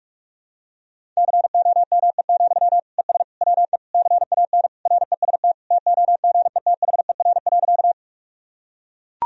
Morse code, heard as a keyed tone, about 41 wpm, 690 Hz. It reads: COME, ES WE CAN REST TOGETHER.
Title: Sentences from Top 500 Words and Abbreviations 45wpm